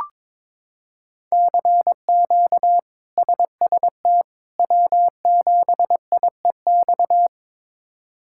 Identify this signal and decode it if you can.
Morse code, heard as a keyed tone, about 22 wpm, 700 Hz. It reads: CQ SST W7IEX
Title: K1USN SST QSO Practice - 22wpm